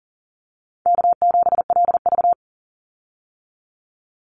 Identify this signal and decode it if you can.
Morse code, heard as a keyed tone, about 40 wpm, 700 Hz. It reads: K7LV
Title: Easy Call Signs - 40wpm